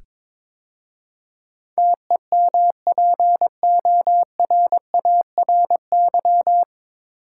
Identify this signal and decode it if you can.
Morse code, heard as a keyed tone, about 22 wpm, 700 Hz. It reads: TEMPORARY
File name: 9 Letter Words - First 3 Letters Given - 22wpm